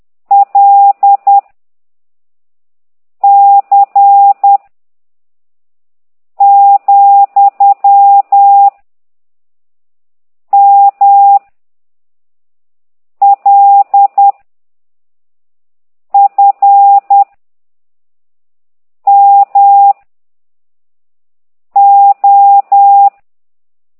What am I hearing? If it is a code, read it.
Morse code, heard as a keyed tone, about 10 wpm, 800 Hz. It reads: L C , M L F M O